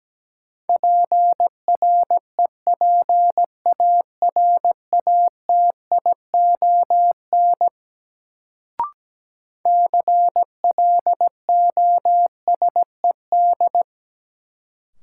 Morse code, heard as a keyed tone, about 17 wpm, 700 Hz. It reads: PREPARATION CLOSED
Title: Sentences from Top 1400 Words - New Word Drill Alt - 17wpm